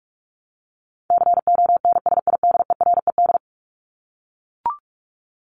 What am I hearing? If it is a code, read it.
Morse code, heard as a keyed tone, about 44 wpm, 700 Hz. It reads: CONSIDERED
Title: ED Suffix - Top 100 Words with Base Words 45wpm